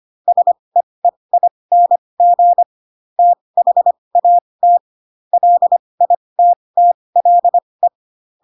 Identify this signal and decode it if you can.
Morse code, heard as a keyed tone, about 25 wpm, spaced as for 20 wpm, 700 Hz. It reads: SEEING THAT LITTLE
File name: Pinocchio / Pinocchio ch7 15 25 ik1hge